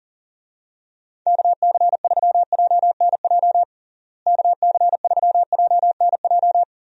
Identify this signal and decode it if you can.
Morse code, heard as a keyed tone, about 37 wpm, 700 Hz. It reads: KC3JNJ KC3JNJ
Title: USA Call Signs - 6 Characters - Repeated 2x - 40wpm